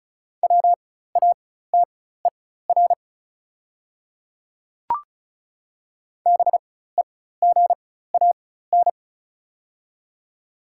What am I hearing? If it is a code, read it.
Morse code, heard as a keyed tone, about 35 wpm, spaced as for 16 wpm, 690 Hz. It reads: WATER BEGAN
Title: 4x Character Spacing - Top 500 Words - 35wpm